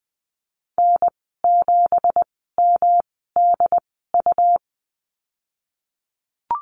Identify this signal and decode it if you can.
Morse code, heard as a keyed tone, about 19 wpm, spaced as for 14 wpm, 700 Hz. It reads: N7MDU
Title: 173 Lesson 28g - Number 9 - Call Signs 20wpm